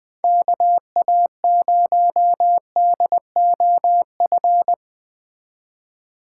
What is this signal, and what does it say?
Morse code, heard as a keyed tone, about 20 wpm, 700 Hz. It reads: KA0DOF
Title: USA Call Signs - 6 Characters - 4x Character Spacing - 20wpm